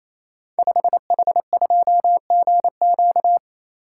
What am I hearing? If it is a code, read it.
Morse code, heard as a keyed tone, about 28 wpm, 690 Hz. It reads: 5H2GQ
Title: Code Groups - 5 Letter:Numbers - 28wpm